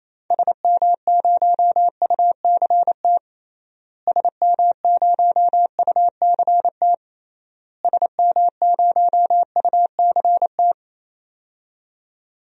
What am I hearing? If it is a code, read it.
Morse code, heard as a keyed tone, about 27 wpm, 700 Hz. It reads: SM0UCT SM0UCT SM0UCT
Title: DXCC-100 Call Signs - 6 Characters - Repeated 3x - 28wpm